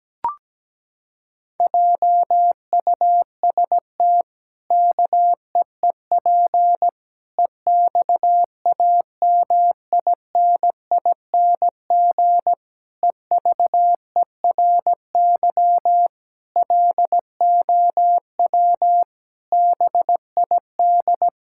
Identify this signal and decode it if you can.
Morse code, heard as a keyed tone, about 17 wpm, 700 Hz. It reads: JUST KEEP EXAMINING EVERY LOW BID
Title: Pangram Challenge v2 - 17wpm